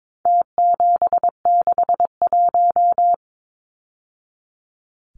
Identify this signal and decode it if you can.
Morse code, heard as a keyed tone, about 22 wpm, 700 Hz. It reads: T761
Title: Numbers - Contest Style - 4-Digit 22wpm